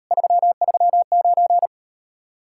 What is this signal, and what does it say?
Morse code, heard as a keyed tone, about 39 wpm, 700 Hz. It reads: 339